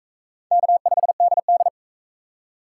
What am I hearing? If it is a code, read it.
Morse code, heard as a keyed tone, about 36 wpm, 700 Hz. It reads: K5DD